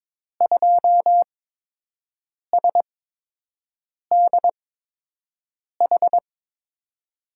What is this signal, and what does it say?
Morse code, heard as a keyed tone, about 22 wpm, spaced as for 6 wpm, 700 Hz. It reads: 2SDH
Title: USA Call Signs - 6 Characters - 8x Character Spacing - 22wpm